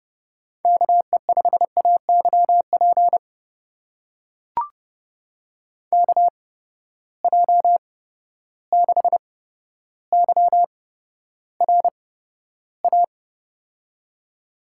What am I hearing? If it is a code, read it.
Morse code, heard as a keyed tone, about 30 wpm, spaced as for 13 wpm, 700 Hz. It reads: KE5AYP KJ6YRA
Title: USA Call Signs - 6 Characters - 8x Character Spacing - 30wpm